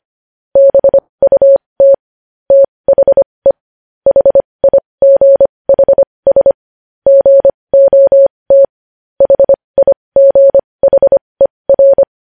Morse code, heard as a keyed tone, about 25 wpm, spaced as for 20 wpm, 550 Hz. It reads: BUT THE HIGHS GOT HIGHER
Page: News Headlines in Morse Code at 20 WPM